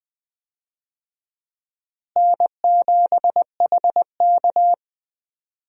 Morse code, heard as a keyed tone, about 20 wpm, 700 Hz. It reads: N7HK
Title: USA Call Signs - 4 Characters - Repeated 2x - 20wpm